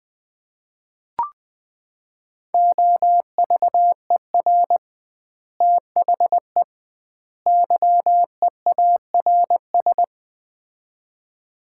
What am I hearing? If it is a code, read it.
Morse code, heard as a keyed tone, about 21 wpm, 700 Hz. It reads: OVER THE YEARS
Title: 2x Word-Spacing - Sets of 3 Words - 20wpm